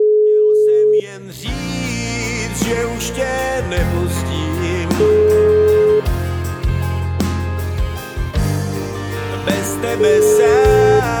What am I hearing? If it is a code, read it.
Ringback tone (Europe, 425 Hz), with speech.